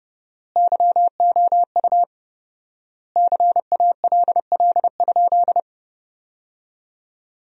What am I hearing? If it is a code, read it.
Morse code, heard as a keyed tone, about 30 wpm, 700 Hz. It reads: YOU CALL?
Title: 4x Word-Spacing - Sentences from Top 100 Words - 30wpm